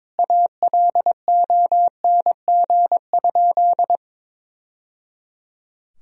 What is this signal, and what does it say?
Morse code, heard as a keyed tone, about 22 wpm, 700 Hz. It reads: ALONG?
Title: Sentences from Top 1000 Words - Review Part 3 - 22wpm